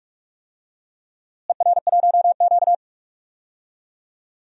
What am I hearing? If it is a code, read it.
Morse code, heard as a keyed tone, about 43 wpm, 700 Hz. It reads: ER1Q
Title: DXCC-100 Call Signs - 4 Characters - Repeated 3x - 45wpm